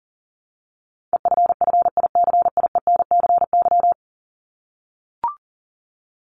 Morse code, heard as a keyed tone, about 40 wpm, 700 Hz. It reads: EFFICIENCY
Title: Top 1501-1750 Words 40wpm